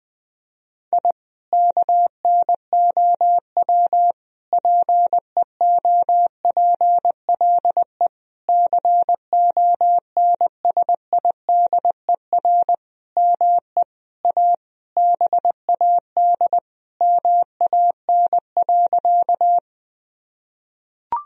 Morse code, heard as a keyed tone, about 20 wpm, 700 Hz. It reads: I KNOW PEOPLE CONSIDER ME A BAD MAN.
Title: Sentences from Top 1000 Words - Review Part 4 - 20wpm